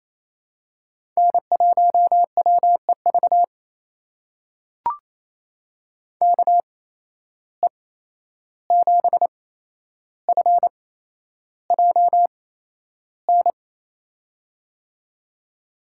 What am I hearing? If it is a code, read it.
Morse code, heard as a keyed tone, about 28 wpm, spaced as for 11 wpm, 700 Hz. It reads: N1WEV KE7FJN
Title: USA Call Signs - 8x Character Spacing - 28wpm